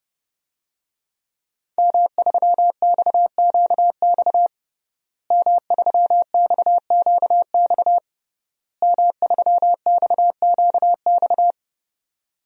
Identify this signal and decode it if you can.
Morse code, heard as a keyed tone, about 30 wpm, 700 Hz. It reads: M3XQX M3XQX M3XQX
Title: DXCC-100 Call Signs - 5 Characters - Repeated 3x - 30wpm